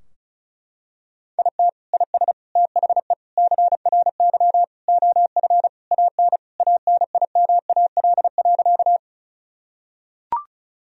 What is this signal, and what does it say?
Morse code, heard as a keyed tone, about 34 wpm, 700 Hz. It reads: IT IS THE CRY OF AN ANIMAL.
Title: Sentences from Top 1000 Words - Review Part 2 - 35wpm